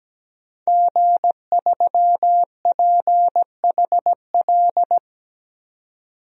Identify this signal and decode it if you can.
Morse code, heard as a keyed tone, about 17 wpm, 700 Hz. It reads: G3PHL
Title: DXCC-100 Call Signs - 5 Characters - 17wpm